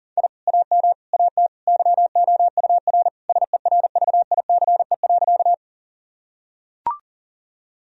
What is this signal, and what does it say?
Morse code, heard as a keyed tone, about 40 wpm, 700 Hz. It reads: I AM AT YOUR SERVICE.
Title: Sentences from Top 1000 Words - Review Part 4 - 40wpm